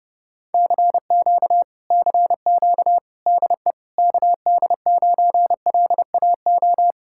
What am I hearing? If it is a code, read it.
Morse code, heard as a keyed tone, about 34 wpm, 700 Hz. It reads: CQ CQ DE KD9LAO